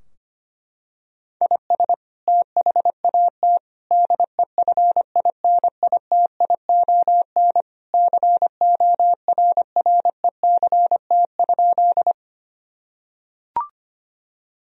Morse code, heard as a keyed tone, about 25 wpm, 700 Hz. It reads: IS THAT DEFINITION CORRECT?
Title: Sentences from Top 1000 Words - Review Part 4 - 25wpm